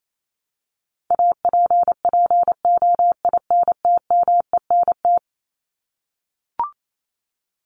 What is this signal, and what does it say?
Morse code, heard as a keyed tone, about 28 wpm, 700 Hz. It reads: APPOINTMENT
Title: Top 1751-2000 Words 28wpm